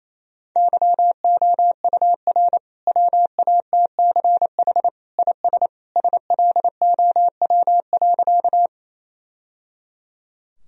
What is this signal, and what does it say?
Morse code, heard as a keyed tone, about 28 wpm, 700 Hz. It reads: YOUR WATCH IS SLOW.
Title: Sentences from Top 1000 Words - Review Part 2 - 28wpm